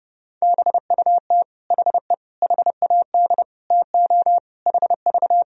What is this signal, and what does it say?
Morse code, heard as a keyed tone, about 32 wpm, 700 Hz. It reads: BUT HE HAD TO HV
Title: Sentences from Top 100 Words and Abbreviations 30wpm